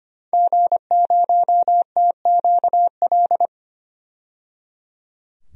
Morse code, heard as a keyed tone, about 25 wpm, 700 Hz. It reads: G0TQL
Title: DXCC-100 Call Signs - 5 Characters - 25wpm